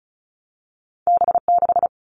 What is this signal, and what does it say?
Morse code, heard as a keyed tone, about 36 wpm, 700 Hz. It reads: B6
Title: All Cut Numbers - 2-Digit 35wpm